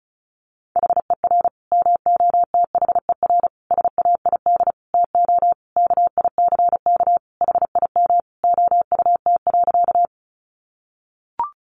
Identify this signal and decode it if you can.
Morse code, heard as a keyed tone, about 34 wpm, 700 Hz. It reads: HER MOTHER SAID TO KICK HIM OUT.
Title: Sentences from Top 1300 Words 35wpm